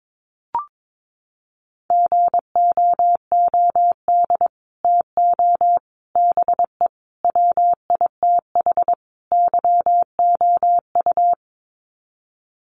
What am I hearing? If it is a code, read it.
Morse code, heard as a keyed tone, about 22 wpm, 700 Hz. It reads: GOOD TO BE WITH YOU
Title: Sets of 5 Words Encore2 22wpm